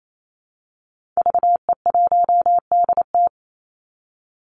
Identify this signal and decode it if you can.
Morse code, heard as a keyed tone, about 28 wpm, 700 Hz. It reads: VE1DT
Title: Call Signs - 28wpm